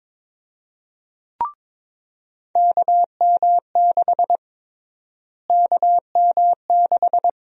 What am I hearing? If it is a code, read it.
Morse code, heard as a keyed tone, about 22 wpm, 700 Hz. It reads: KM6 KM6
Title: USA Call Sign Prefixes - Repeated 2x - 22wpm